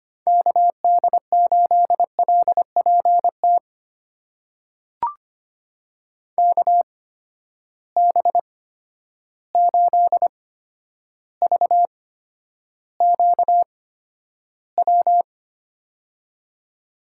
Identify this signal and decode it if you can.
Morse code, heard as a keyed tone, about 25 wpm, spaced as for 11 wpm, 700 Hz. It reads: KD8LPT KB8VQW
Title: USA Call Signs - 8x Character Spacing - 25wpm